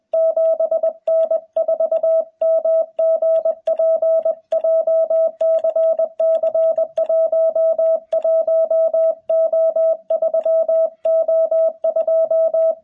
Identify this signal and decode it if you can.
Morse code, heard as a keyed tone, about 20 wpm, 650 Hz, with radio noise. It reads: 7N4MGP JCC11O3O2
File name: jh8cbh / 2021cw